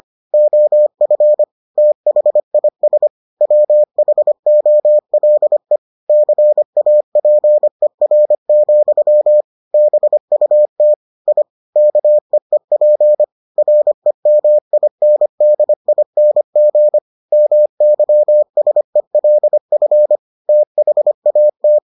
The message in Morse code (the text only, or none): OF THIS WHOLE CAPER, BUT I KEEP REMINDING MYSELF THAT